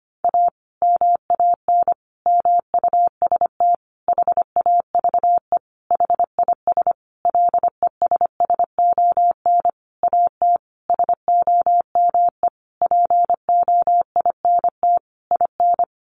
Morse code, heard as a keyed tone, about 25 wpm, 700 Hz. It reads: A MAN MUST HAVE HIS LESSON AT SOME POINT IN
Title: Sentences from Top 1200 Words 25wpm